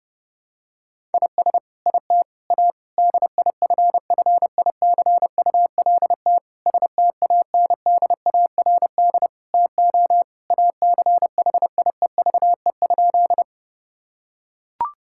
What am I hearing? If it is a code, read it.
Morse code, heard as a keyed tone, about 30 wpm, 700 Hz. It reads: IS IT A DIFFICULT STANDARD TO ACHIEVE?
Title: Sentences from Top 1000 Words - Review Part 6 - 30wpm